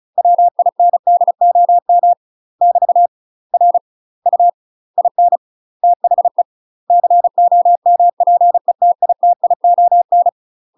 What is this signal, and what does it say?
Morse code, heard as a keyed tone, about 36 wpm, 700 Hz. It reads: WINDOM <BT> R U IN THE COMPETITION
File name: UK MASTER QSOs TRAINING / 35WPM / UK MASTER QSOs 0013